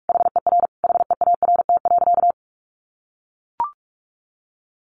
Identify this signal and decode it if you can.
Morse code, heard as a keyed tone, about 44 wpm, 700 Hz. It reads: HER HEART.
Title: Sentences from Top 1200 Words 45wpm